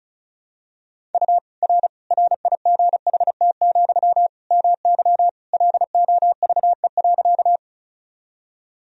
Morse code, heard as a keyed tone, about 34 wpm, 700 Hz. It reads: U R RIGHT, MY LOVE.
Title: Sentences from Top 400 Words and Abbreviations 35wpm